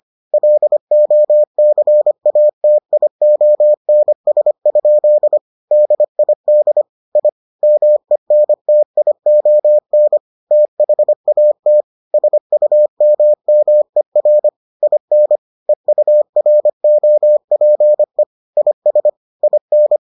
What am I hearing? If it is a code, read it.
Morse code, heard as a keyed tone, about 25 wpm, 600 Hz. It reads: LOCATIONS? DID I MENTION THAT SUMMER IN EUROPE IS IN